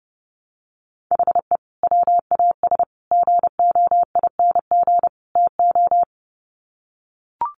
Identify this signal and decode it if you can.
Morse code, heard as a keyed tone, about 30 wpm, 700 Hz. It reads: HE WAS GOING TO
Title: Sets of 4 Words 30wpm